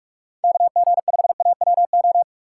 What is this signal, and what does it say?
Morse code, heard as a keyed tone, about 46 wpm, 700 Hz. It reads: KG5AWO
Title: USA Call Signs - 6 Characters - 6x Character Spacing - 45wpm